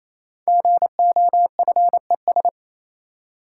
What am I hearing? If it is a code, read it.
Morse code, heard as a keyed tone, about 27 wpm, 700 Hz. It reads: GOFES
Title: Code Groups - 5 Letters - 28wpm